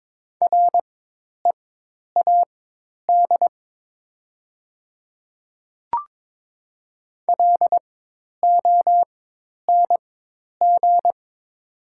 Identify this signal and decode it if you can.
Morse code, heard as a keyed tone, about 22 wpm, spaced as for 10 wpm, 700 Hz. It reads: READ LONG
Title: 4x Character Spacing - Top 200 Words - 22wpm